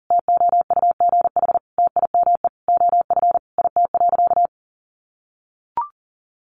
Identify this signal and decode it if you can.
Morse code, heard as a keyed tone, about 40 wpm, 700 Hz. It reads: TOUGH TIME OF IT.
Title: Sentences from Top 1200 Words 40wpm